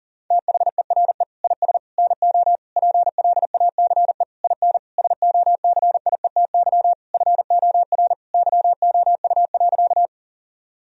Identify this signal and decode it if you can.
Morse code, heard as a keyed tone, about 40 wpm, 700 Hz. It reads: THERE IS NO PLACE IN SOCIETY FOR YOU.
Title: Sentences from Top 1000 Words - Review Part 4 - 40wpm